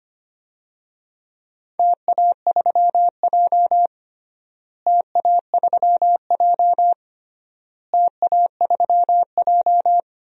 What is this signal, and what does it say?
Morse code, heard as a keyed tone, about 25 wpm, 700 Hz. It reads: TA3J TA3J TA3J
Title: DXCC-100 Call Signs - 4 Characters - Repeated 3x - 25wpm